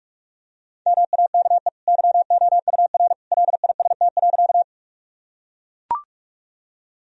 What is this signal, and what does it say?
Morse code, heard as a keyed tone, about 41 wpm, 700 Hz. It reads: MAKE YOUR LIST.
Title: Sentences from Top 1000 Words - Review Part 2 - 45wpm